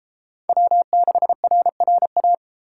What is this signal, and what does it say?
Morse code, heard as a keyed tone, about 30 wpm, 700 Hz. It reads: W6RRA